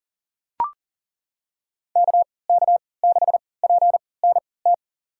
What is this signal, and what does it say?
Morse code, heard as a keyed tone, about 40 wpm, spaced as for 23 wpm, 700 Hz. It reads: KK6PNT
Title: USA Call Signs - 3x Character Spacing - 40wpm